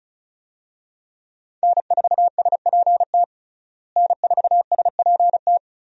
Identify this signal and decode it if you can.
Morse code, heard as a keyed tone, about 34 wpm, 700 Hz. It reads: N4SPT N4SPT
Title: USA Call Signs - 5 Characters - Repeated 2x - 35wpm